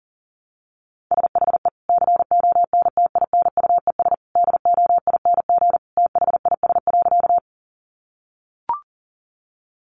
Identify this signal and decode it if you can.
Morse code, heard as a keyed tone, about 40 wpm, 700 Hz. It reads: SHE CONTINUES DOING THIS.
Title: Easy Phrases 40wpm